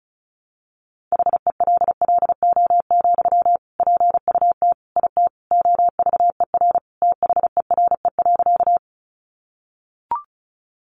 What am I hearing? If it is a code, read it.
Morse code, heard as a keyed tone, about 34 wpm, 700 Hz. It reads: HELLO, PUT IT OVER THERE.
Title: Sentences from Top 1100 Words 35wpm